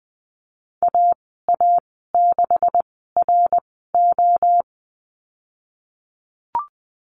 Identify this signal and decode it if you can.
Morse code, heard as a keyed tone, about 20 wpm, spaced as for 14 wpm, 700 Hz. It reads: AA6RO
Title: 154 Lesson 25g - Letter p - Call Signs 20wpm